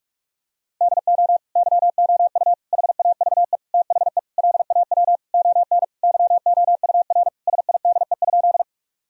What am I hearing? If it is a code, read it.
Morse code, heard as a keyed tone, about 46 wpm, 700 Hz. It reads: DO YOU HAVE THE LAW ON YOUR SIDE?
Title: Sentences from Top 1000 Words - Review Part 3 - 45wpm